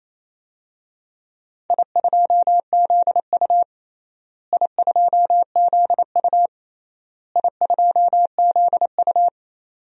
Morse code, heard as a keyed tone, about 28 wpm, 700 Hz. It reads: I2ZU I2ZU I2ZU
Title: DXCC-100 Call Signs - 4 Characters - Repeated 3x - 28wpm